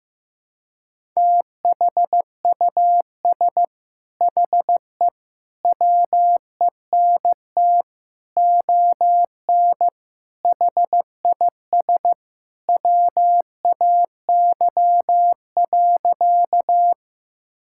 Morse code, heard as a keyed tone, about 15 wpm, 700 Hz. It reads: THUS HE WENT ON HIS WAY.
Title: Sentences from Top 1000 Words - Review Part 5 - 15wpm